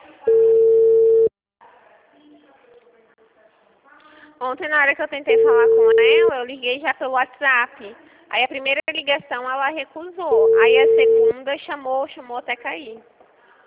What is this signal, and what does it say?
Ringback tone (China, 450 Hz), with speech.
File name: out- -2024- - - .22832